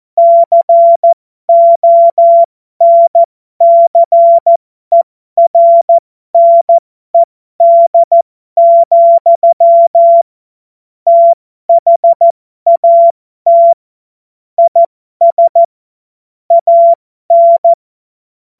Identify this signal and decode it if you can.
Morse code, heard as a keyed tone, about 14 wpm, spaced as for 12 wpm, 680 Hz. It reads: CONCERNED, THAT IS AN